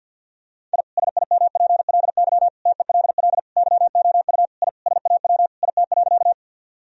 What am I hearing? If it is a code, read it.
Morse code, heard as a keyed tone, about 50 wpm, 700 Hz. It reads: I SIMPLY TELL YOU I SAW IT.
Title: Sentences from Top 1000 Words - Review Part 4 - 50wpm